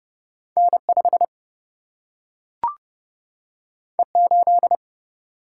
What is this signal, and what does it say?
Morse code, heard as a keyed tone, about 30 wpm, 700 Hz. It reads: N5 E8
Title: Single-Letter Single-Number 30wpm